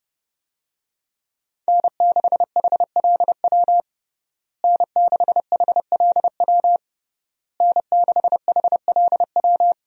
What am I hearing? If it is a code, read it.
Morse code, heard as a keyed tone, about 30 wpm, 700 Hz. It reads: N6HLW N6HLW N6HLW
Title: USA Call Signs - 5 Characters - Repeated 3x - 30wpm